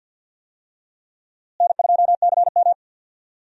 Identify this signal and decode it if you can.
Morse code, heard as a keyed tone, about 52 wpm, 700 Hz. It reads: N2CM
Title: USA Call Signs - 4 Characters - Repeated 3x - 50wpm